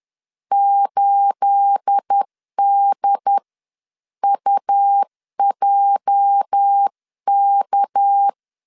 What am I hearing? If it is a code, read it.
Morse code, heard as a keyed tone, about 10 wpm, spaced as for 9 wpm, 800 Hz. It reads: 8DUJK
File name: LEZIONI / 9 28 10wpm 2842